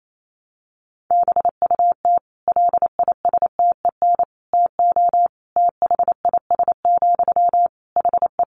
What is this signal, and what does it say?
Morse code, heard as a keyed tone, about 28 wpm, 700 Hz. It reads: BUT LISTEN TO THIS, HE
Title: Sentences from Top 400 Words 28wpm